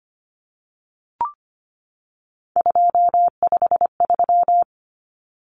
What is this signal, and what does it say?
Morse code, heard as a keyed tone, about 25 wpm, 700 Hz. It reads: 253
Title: Numbers - 3-Digit 25wpm